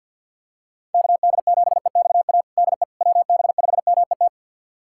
Morse code, heard as a keyed tone, about 52 wpm, 700 Hz. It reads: KD7EXA DE WB5MET
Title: USA Callsign DE Callsign - 6 Characters - 50wpm